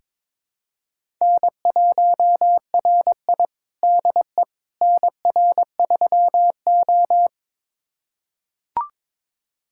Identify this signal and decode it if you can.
Morse code, heard as a keyed tone, about 21 wpm, 700 Hz. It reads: N1RI DE NR3O
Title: USA Callsign DE Callsign - 4 Characters - 22wpm